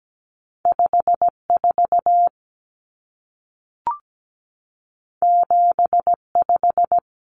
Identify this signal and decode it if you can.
Morse code, heard as a keyed tone, about 17 wpm, 700 Hz. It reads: 54 75
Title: Numbers - 2-Digit 17wpm